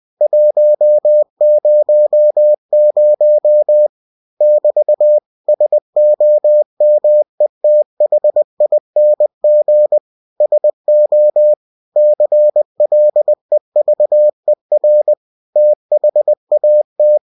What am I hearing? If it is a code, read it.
Morse code, heard as a keyed tone, about 20 wpm, 600 Hz. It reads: 100 <BT> SOMETHING SO CLEVER THAT